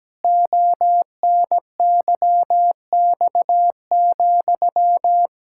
Morse code, heard as a keyed tone, about 17 wpm, 700 Hz. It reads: ONYX,
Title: Pangram Challenge v2 - 17wpm